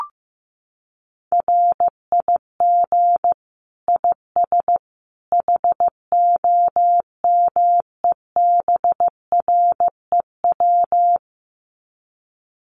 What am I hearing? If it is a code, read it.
Morse code, heard as a keyed tone, about 15 wpm, 700 Hz. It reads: RIG IS HOMEBREW
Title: Common QSO Phrases 15wpm